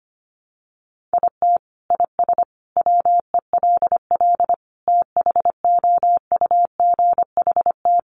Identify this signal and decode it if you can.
Morse code, heard as a keyed tone, about 25 wpm, 700 Hz. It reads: IT IS WELL THOUGHT
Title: Sentences from Top 300 Words 25wpm